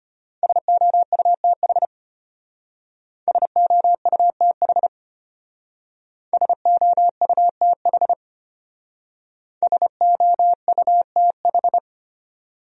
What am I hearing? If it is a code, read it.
Morse code, heard as a keyed tone, about 31 wpm, 700 Hz. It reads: SOUTH SOUTH SOUTH SOUTH
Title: Speed-Racing Top 500 Words 25wpm